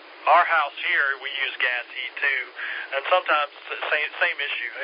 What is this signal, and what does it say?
Emergency Alert System (SAME) tones, heard as harsh buzzing bursts, with speech. It is heard from 0.8 to 4.4 s.